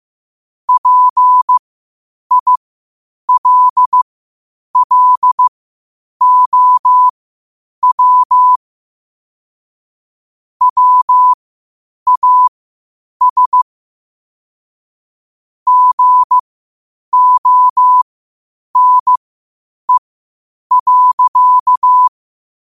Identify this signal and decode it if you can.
Morse code, heard as a keyed tone, about 15 wpm, spaced as for 8 wpm, 1000 Hz. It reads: PILLOW WAS GONE.
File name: qotdfiles / 8wpm